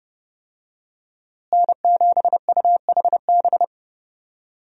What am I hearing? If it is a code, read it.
Morse code, heard as a keyed tone, about 30 wpm, 700 Hz. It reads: N7UHB
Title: USA Call Signs - 5 Characters - Repeated 3x - 30wpm